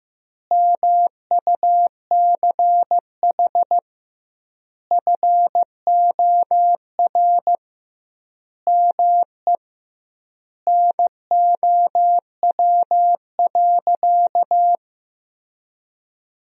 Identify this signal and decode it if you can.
Morse code, heard as a keyed tone, about 15 wpm, 700 Hz. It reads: MUCH FOR ME NOW.
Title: 2x Word-Spacing - Sentences from Top 200 Words - 15wpm